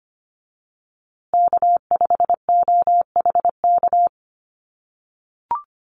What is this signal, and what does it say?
Morse code, heard as a keyed tone, about 25 wpm, 700 Hz. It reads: K5OHK
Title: USA Call Signs Repeated 2x 25wpm